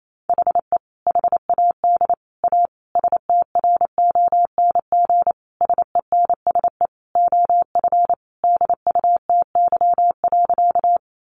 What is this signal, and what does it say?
Morse code, heard as a keyed tone, about 28 wpm, 700 Hz. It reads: HE HAD A STRONG SENSE OF DUTY.
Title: Sentences from Top 1500 Words 28wpm